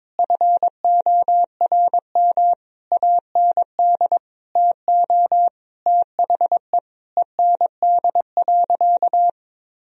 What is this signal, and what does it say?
Morse code, heard as a keyed tone, about 22 wpm, 700 Hz. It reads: FORM AND TO THE END.
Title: Sentences from Top 1000 Words - Review Part 1 - 22wpm